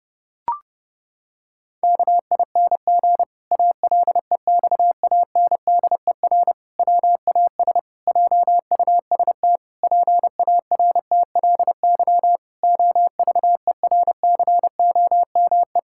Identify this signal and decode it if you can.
Morse code, heard as a keyed tone, about 30 wpm, 700 Hz. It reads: KING ALEXANDER WAS JUST PARTLY OVERCOME
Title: Pangram Challenge v2 - 30wpm